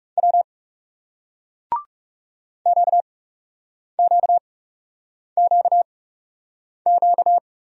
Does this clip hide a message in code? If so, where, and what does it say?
Yes: Morse code from 0.2 to 7.4 s, heard as a keyed tone, about 38 wpm, 700 Hz. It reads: W Q Q Q Q